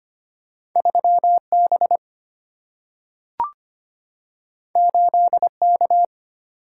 Morse code, heard as a keyed tone, about 25 wpm, 700 Hz. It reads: 3 B 8 K
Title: Single-Letter Single-Number 25wpm